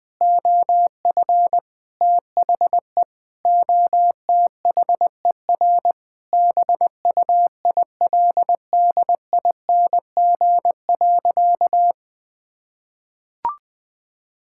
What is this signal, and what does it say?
Morse code, heard as a keyed tone, about 20 wpm, 700 Hz. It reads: OF THE OTHER BUILDING.
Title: Sentences with 3-Word Phrases - 20wpm